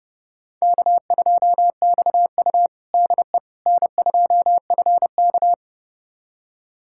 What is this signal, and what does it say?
Morse code, heard as a keyed tone, about 30 wpm, 700 Hz. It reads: K2XU DE N2FK
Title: USA Callsign DE Callsign - 4 Characters - 30wpm